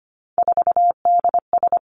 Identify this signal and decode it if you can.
Morse code, heard as a keyed tone, about 25 wpm, 700 Hz. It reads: 4DS
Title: Three Challenging Letter-Number Combinations 25wpm